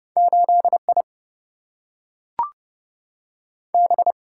Morse code, heard as a keyed tone, about 31 wpm, 700 Hz. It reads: 8 I B
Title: Single-Letter Single-Number 30wpm